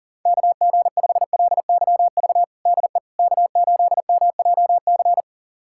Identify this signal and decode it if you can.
Morse code, heard as a keyed tone, about 40 wpm, 700 Hz. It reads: KG5LYV DE K8MJC
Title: USA Callsign DE Callsign 40wpm